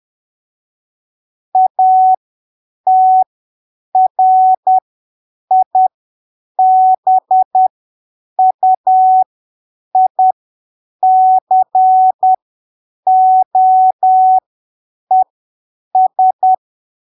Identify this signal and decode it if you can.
Morse code, heard as a keyed tone, about 10 wpm, spaced as for 7 wpm, 750 Hz. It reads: ATRIBUICOES
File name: arquivos / texto / texto 12 10ppm